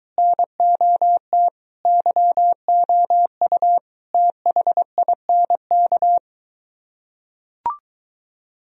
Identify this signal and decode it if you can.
Morse code, heard as a keyed tone, about 23 wpm, 700 Hz. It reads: NOT YOU THINK